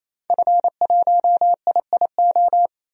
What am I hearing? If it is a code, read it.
Morse code, heard as a keyed tone, about 28 wpm, 700 Hz. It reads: F1IIO
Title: DXCC-100 Call Signs - 5 Characters - 28wpm